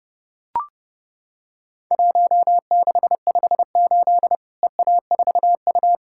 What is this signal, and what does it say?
Morse code, heard as a keyed tone, about 30 wpm, 700 Hz. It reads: 1658 EA4U
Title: Worked All Europe DX Contest - QTC Exchange - 30wpm